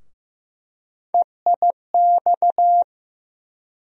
Morse code, heard as a keyed tone, about 15 wpm, 700 Hz. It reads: EIX
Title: USA Call Sign Postfixes - 3 Letters 15wpm